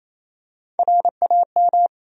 Morse code, heard as a keyed tone, about 28 wpm, 700 Hz. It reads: RAM
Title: USA Call Sign Postfixes - 3 Letters 28wpm